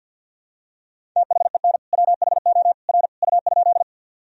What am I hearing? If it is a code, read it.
Morse code, heard as a keyed tone, about 50 wpm, 700 Hz. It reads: THEN WHO R U?
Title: Sentences from Top 100 Words and Abbreviations 50wpm